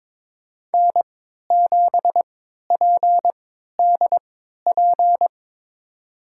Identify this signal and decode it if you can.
Morse code, heard as a keyed tone, about 22 wpm, spaced as for 13 wpm, 700 Hz. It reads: N7PDP
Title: USA Call Signs - 5 Characters - 3x Character Spacing - 22wpm